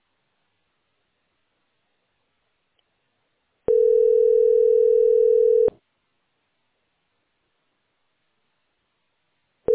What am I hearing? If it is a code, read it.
Ringback tone (North America), 440 + 480 Hz.